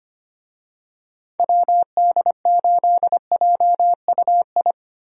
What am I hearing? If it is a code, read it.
Morse code, heard as a keyed tone, about 25 wpm, 700 Hz. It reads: WD8JUI
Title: USA Call Signs - 6 Characters - Repeated 3x - 25wpm